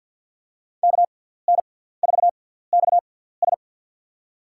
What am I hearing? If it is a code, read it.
Morse code, heard as a keyed tone, about 52 wpm, spaced as for 17 wpm, 700 Hz. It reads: KN4XS
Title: USA Call Signs - 6x Character Spacing - 50wpm